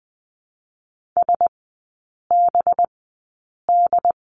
Morse code, heard as a keyed tone, about 20 wpm, spaced as for 8 wpm, 700 Hz. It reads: SBD
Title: 184 Lesson 30e - Letter b - Three Characters 20wpm